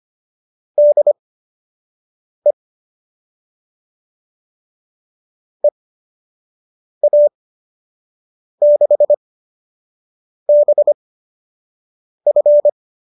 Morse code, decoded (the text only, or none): DE EA6BF